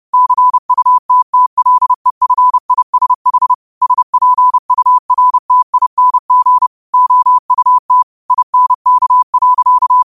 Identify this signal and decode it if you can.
Morse code, heard as a keyed tone, about 30 wpm, 1000 Hz. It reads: CUTTLEFISH SPURTING OUT INK.